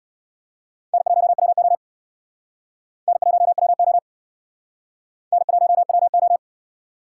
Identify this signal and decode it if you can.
Morse code, heard as a keyed tone, about 66 wpm, 700 Hz. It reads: N1WO N1WO N1WO